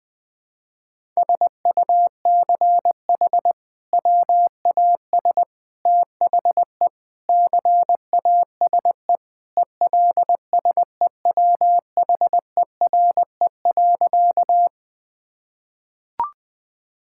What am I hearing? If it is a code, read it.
Morse code, heard as a keyed tone, about 20 wpm, 700 Hz. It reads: SUCH WAS THE CASE ELSEWHERE.
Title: Sentences from Top 1600 Words 20wpm